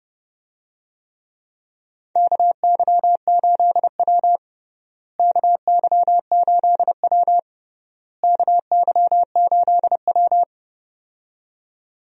Morse code, heard as a keyed tone, about 30 wpm, 700 Hz. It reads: KY8W KY8W KY8W
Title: USA Call Signs - 4 Characters - Repeated 3x - 30wpm